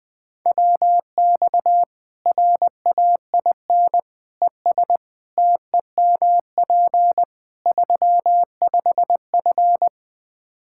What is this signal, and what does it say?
Morse code, heard as a keyed tone, about 20 wpm, 700 Hz. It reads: WX RAIN ES TEMP 35F
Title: Practice QSO - Exchange 2 - Info - 20wpm